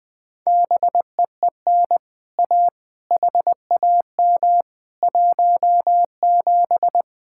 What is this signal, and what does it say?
Morse code, heard as a keyed tone, about 21 wpm, 700 Hz. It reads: BEEN A HAM 17
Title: Speed-Racing Common QSO Phrases - 20wpm